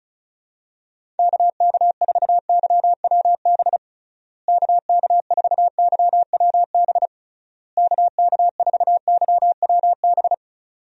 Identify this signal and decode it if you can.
Morse code, heard as a keyed tone, about 36 wpm, 700 Hz. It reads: KK4YWB KK4YWB KK4YWB
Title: USA Call Signs - 6 Characters - Repeated 3x - 35wpm